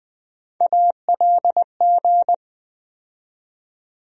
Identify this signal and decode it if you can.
Morse code, heard as a keyed tone, about 20 wpm, 700 Hz. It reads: ALG
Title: USA Call Sign Postfixes - 3 Letters 20wpm